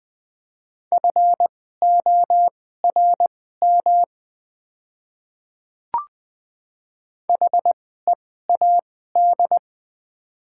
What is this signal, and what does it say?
Morse code, heard as a keyed tone, about 20 wpm, spaced as for 15 wpm, 700 Hz. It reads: FORM HEAD
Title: 2x Character Spacing - Top 300 Words - 20wpm